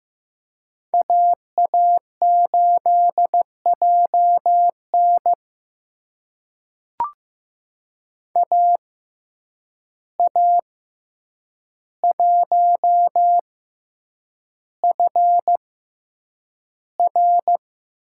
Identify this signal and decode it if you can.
Morse code, heard as a keyed tone, about 15 wpm, spaced as for 8 wpm, 700 Hz. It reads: AA8JN AA1FR
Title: USA Call Signs - 5 Characters - 6x Character Spacing - 15wpm